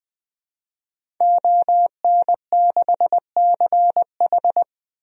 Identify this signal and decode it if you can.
Morse code, heard as a keyed tone, about 21 wpm, 700 Hz. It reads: ON6CH
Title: DXCC-100 Call Signs - 5 Characters - Repeated 3x - 20wpm